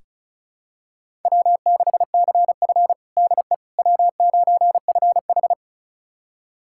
Morse code, heard as a keyed tone, about 34 wpm, 700 Hz. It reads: W6CF DE W9FH
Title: USA Callsign DE Callsign - 4 Characters - 35wpm